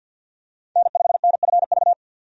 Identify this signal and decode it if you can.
Morse code, heard as a keyed tone, about 46 wpm, 700 Hz. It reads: N5NFV
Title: USA Call Signs - 5 Characters - 50wpm